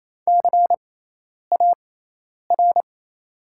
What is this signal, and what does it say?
Morse code, heard as a keyed tone, about 28 wpm, spaced as for 10 wpm, 700 Hz. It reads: CAR
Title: 6x Character Spacing - Top 500 Words - 28wpm